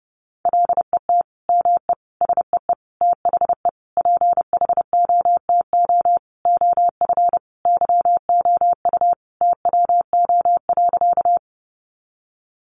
Morse code, heard as a keyed tone, about 28 wpm, 700 Hz. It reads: LET ME SEE THE PHOTO OF YOU TWO.
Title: Sentences from Top 1200 Words 30wpm